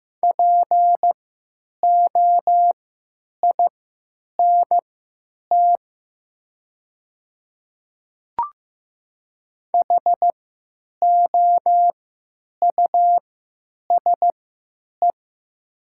Morse code, heard as a keyed tone, about 15 wpm, spaced as for 9 wpm, 700 Hz. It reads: POINT HOUSE
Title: 3x Character Spacing - Top 200 Words - 15wpm